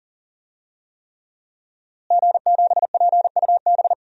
Morse code, heard as a keyed tone, about 37 wpm, 700 Hz. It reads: G7PUB
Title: DXCC-100 Call Signs - 5 Characters - Repeated 3x - 40wpm